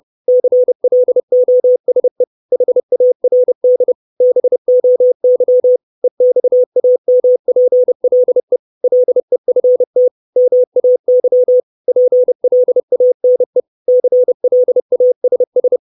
Morse code, heard as a keyed tone, about 30 wpm, 500 Hz. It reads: CLOSE HARD BOY EXAMPLE LEFT MAY PLANE CLASS
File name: 632 Words 30WPM